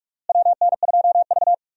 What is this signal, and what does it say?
Morse code, heard as a keyed tone, about 46 wpm, 700 Hz. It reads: WN2V
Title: USA Call Signs - 4 Characters - Repeated 3x - 45wpm